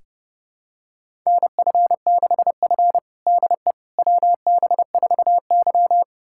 Morse code, heard as a keyed tone, about 32 wpm, 700 Hz. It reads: NF6F DE WB4Y
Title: USA Callsign DE Callsign - 4 Characters - 30wpm